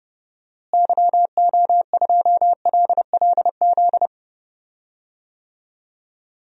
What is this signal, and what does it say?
Morse code, heard as a keyed tone, about 31 wpm, 700 Hz. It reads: YO2LLZ
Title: DXCC-100 Call Signs - 6 Characters - Repeated 2x - 30wpm